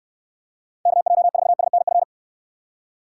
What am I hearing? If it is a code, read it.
Morse code, heard as a keyed tone, about 66 wpm, 700 Hz. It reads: KJ4SNF